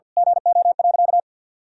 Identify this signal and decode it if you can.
Morse code, heard as a keyed tone, about 52 wpm, 700 Hz. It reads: GO.
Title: Sentences from Top 200 Words and Abbreviations 50wpm